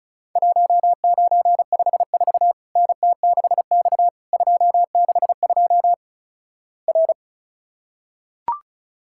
Morse code, heard as a keyed tone, about 36 wpm, 700 Hz. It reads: 1954 NT6X 262 R
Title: Worked All Europe DX Contest - QTC Exchange - 35wpm